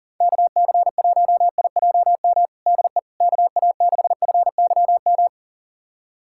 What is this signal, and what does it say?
Morse code, heard as a keyed tone, about 40 wpm, 700 Hz. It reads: KC1IJM DE KA6FYM
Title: USA Callsign DE Callsign - 6 Characters - 40wpm